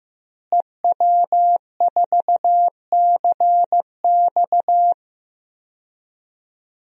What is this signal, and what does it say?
Morse code, heard as a keyed tone, about 16 wpm, 700 Hz. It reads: EW4CX
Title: DXCC-100 Call Signs - 5 Characters - Repeated 2x - 15wpm